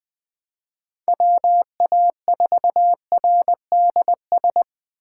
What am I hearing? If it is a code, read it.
Morse code, heard as a keyed tone, about 20 wpm, 700 Hz. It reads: WA4RDS
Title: USA Call Signs - 6 Characters - 6x Character Spacing - 20wpm